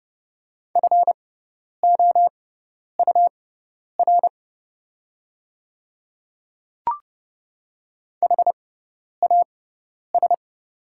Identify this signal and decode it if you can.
Morse code, heard as a keyed tone, about 30 wpm, spaced as for 10 wpm, 700 Hz. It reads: FOUR HAS
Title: 6x Character Spacing - Top 300 Words - 30wpm